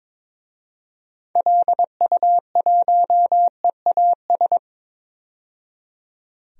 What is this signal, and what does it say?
Morse code, heard as a keyed tone, about 22 wpm, 700 Hz. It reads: LU1EAS
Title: DXCC-100 Call Signs - 6 Characters - 22wpm